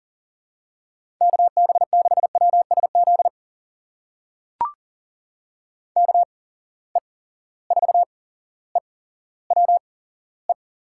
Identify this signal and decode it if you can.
Morse code, heard as a keyed tone, about 39 wpm, spaced as for 17 wpm, 700 Hz. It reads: KB6WSZ KE4EWE
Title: USA Call Signs - 6 Characters - 8x Character Spacing - 40wpm